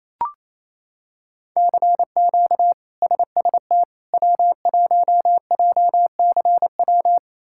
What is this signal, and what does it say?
Morse code, heard as a keyed tone, about 28 wpm, 700 Hz. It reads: CQ SST W1JCW
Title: K1USN SST QSO Practice - 28wpm